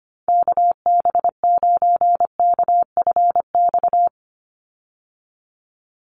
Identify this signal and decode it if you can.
Morse code, heard as a keyed tone, about 25 wpm, 700 Hz. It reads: KB9KFX
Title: USA Call Signs 25wpm